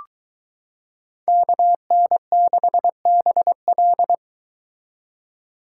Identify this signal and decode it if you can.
Morse code, heard as a keyed tone, about 22 wpm, 700 Hz. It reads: KN6BL